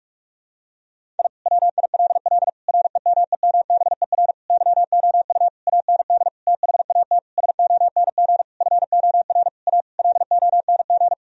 Morse code, heard as a keyed tone, about 46 wpm, 700 Hz. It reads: I WILL REMEMBER YOU AND THAT SONG FOR A LONG
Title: Sentences from Top 1000 Words - Review Part 2 - 45wpm